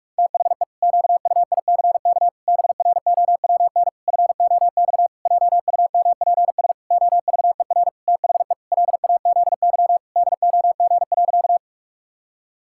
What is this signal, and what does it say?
Morse code, heard as a keyed tone, about 43 wpm, 700 Hz. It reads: THE QUICK BROWN FOX JUMPS OVER THE LAZY DOG.
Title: Pangram Challenge v2 - 45wpm